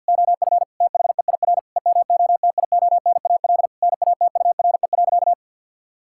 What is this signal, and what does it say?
Morse code, heard as a keyed tone, about 51 wpm, 700 Hz. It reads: OF THEIR EMOTIONAL NATURE.
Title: Sentences from Top 1000 Words - Review Part 5 - 50wpm